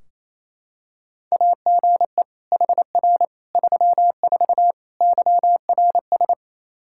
Morse code, heard as a keyed tone, about 28 wpm, 700 Hz. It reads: AGE HR 34 YRS
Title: Practice QSO - Exchange 3 - More Info - 28wpm